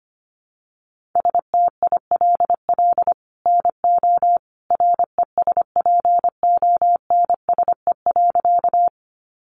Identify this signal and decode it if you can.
Morse code, heard as a keyed tone, about 25 wpm, 700 Hz. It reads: STILL NO RESPONSE.
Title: Sentences from Top 1400 Words 25wpm